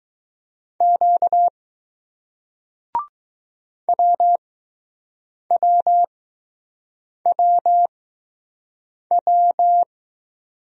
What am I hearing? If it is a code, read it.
Morse code, heard as a keyed tone, about 20 wpm, spaced as for 6 wpm, 700 Hz. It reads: Q WWWW